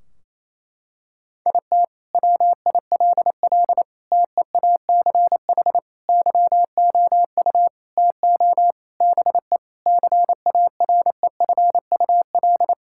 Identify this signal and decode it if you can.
Morse code, heard as a keyed tone, about 28 wpm, 700 Hz. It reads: IT WILL TEACH YOU TO BE CAREFUL
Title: Sentences from Top 1600 Words 28wpm